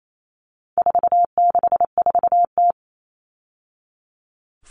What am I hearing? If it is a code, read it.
Morse code, heard as a keyed tone, about 28 wpm, 700 Hz. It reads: 464T
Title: All Cut Numbers - 4-Digit 28wpm